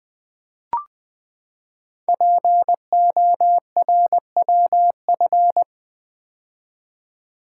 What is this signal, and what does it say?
Morse code, heard as a keyed tone, about 20 wpm, 700 Hz. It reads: PORWF
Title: Code Groups - 5 Letters - 20wpm